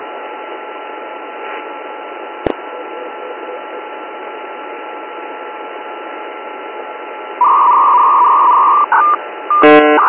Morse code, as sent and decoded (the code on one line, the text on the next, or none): none